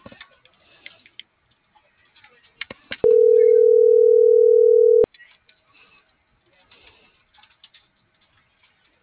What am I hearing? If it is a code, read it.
Ringback tone (North America), 440 + 480 Hz.